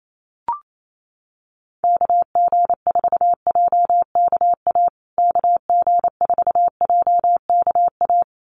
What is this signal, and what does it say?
Morse code, heard as a keyed tone, about 28 wpm, 700 Hz. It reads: KG4JKA KG4JKA
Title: USA Call Signs Repeated 2x 28wpm